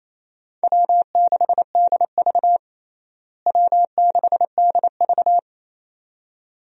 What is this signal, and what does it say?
Morse code, heard as a keyed tone, about 27 wpm, 700 Hz. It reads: W6DV W6DV
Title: USA Call Signs - 4 Characters - Repeated 2x - 28wpm